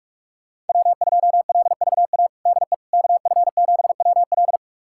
Morse code, heard as a keyed tone, about 46 wpm, 700 Hz. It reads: W2LVA DE KF7WL
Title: USA Callsign DE Callsign - 5 Characters - 45wpm